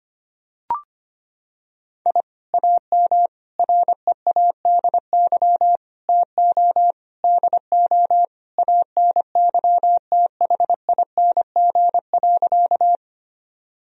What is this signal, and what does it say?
Morse code, heard as a keyed tone, about 25 wpm, 700 Hz. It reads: I AM READY TO DO ANYTHING.
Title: Sentences from Top 1000 Words - Review Part 2 - 25wpm